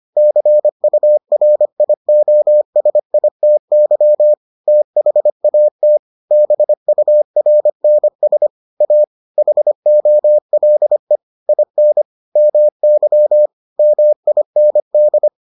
Morse code, heard as a keyed tone, about 25 wpm, 600 Hz. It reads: CURIOSITY THAT BURNS A HOLE IN MY MIND